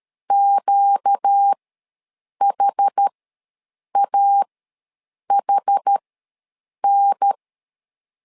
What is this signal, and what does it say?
Morse code, heard as a keyed tone, about 13 wpm, spaced as for 7 wpm, 800 Hz. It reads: QHAHN
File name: LEZIONI / 4 Q ws 12wpm 1912